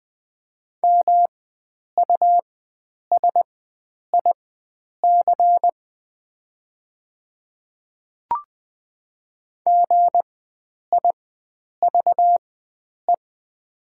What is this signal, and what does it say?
Morse code, heard as a keyed tone, about 20 wpm, spaced as for 9 wpm, 700 Hz. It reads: MUSIC GIVE
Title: 4x Character Spacing - Top 300 Words - 20wpm